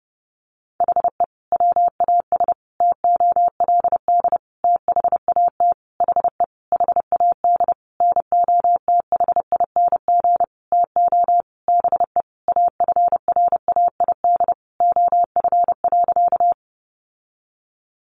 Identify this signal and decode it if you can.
Morse code, heard as a keyed tone, about 30 wpm, 700 Hz. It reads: HE WAS TOLD THAT HE HAD NOTHING TO BE AFRAID OF.
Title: Sentences from Top 1500 Words 30wpm